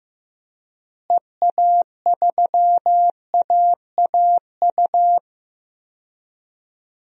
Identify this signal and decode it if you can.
Morse code, heard as a keyed tone, about 15 wpm, 700 Hz. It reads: EA3AAU
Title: DXCC-100 Call Signs - 6 Characters - Repeated 3x - 15wpm